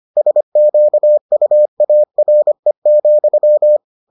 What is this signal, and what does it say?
Morse code, heard as a keyed tone, about 25 wpm, 600 Hz. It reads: SQUARE,